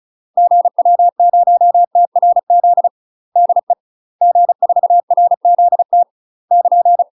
Morse code, heard as a keyed tone, about 34 wpm, 700 Hz. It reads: GW0TRZ DE G4RZT <KN>
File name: UK MASTER QSOs TRAINING / 35WPM / UK MASTER QSOs 0004